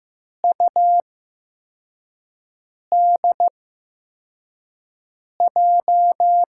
Morse code, heard as a keyed tone, about 15 wpm, spaced as for 4 wpm, 700 Hz. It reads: UDJ
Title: USA Call Signs - 5 Characters - 8x Character Spacing - 15wpm